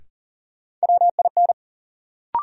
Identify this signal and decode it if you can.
Morse code, heard as a keyed tone, about 40 wpm, 700 Hz. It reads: WIN
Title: Sentences from Top 1500 Words - New Word Drill Alt - 40wpm